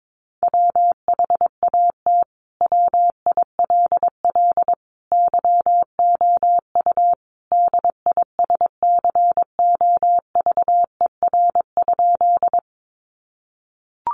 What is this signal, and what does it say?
Morse code, heard as a keyed tone, about 22 wpm, 700 Hz. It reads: WHAT WILL YOU DISCOVER?
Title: Sentences from Top 1100 Words 22wpm